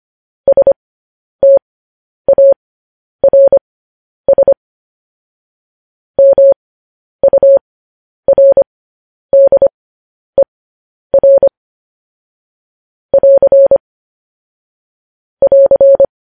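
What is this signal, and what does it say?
Morse code, heard as a keyed tone, about 25 wpm, spaced as for 10 wpm, 550 Hz. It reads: STARS MURDER <AR> <AR>
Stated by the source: Morse code transmission of news headlines.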